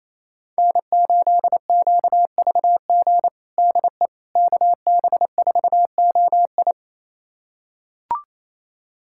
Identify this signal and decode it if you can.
Morse code, heard as a keyed tone, about 28 wpm, 700 Hz. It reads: N8QVG DE KB4OI
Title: USA Callsign DE Callsign - 5 Characters - 28wpm